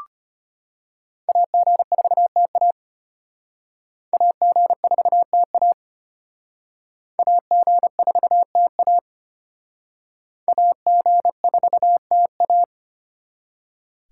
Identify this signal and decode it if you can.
Morse code, heard as a keyed tone, about 31 wpm, 700 Hz. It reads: AG4TA AG4TA AG4TA AG4TA
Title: Speed Racing - USA 2x2 Call Signs 25wpm